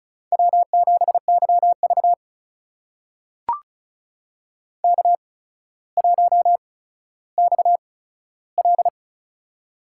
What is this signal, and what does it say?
Morse code, heard as a keyed tone, about 36 wpm, spaced as for 15 wpm, 700 Hz. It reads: W7YV K1XL
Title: USA Call Signs - 4 Characters - 8x Character Spacing - 35wpm